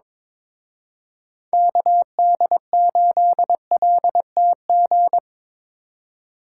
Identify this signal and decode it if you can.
Morse code, heard as a keyed tone, about 24 wpm, 700 Hz. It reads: KD8LTG
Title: Speed Racing - USA 2x3 Call Signs 22wpm